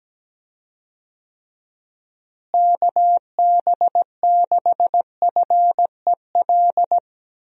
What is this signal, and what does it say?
Morse code, heard as a keyed tone, about 17 wpm, 700 Hz. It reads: KB6FEL
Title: USA Call Signs - 6 Characters - Repeated 2x - 17wpm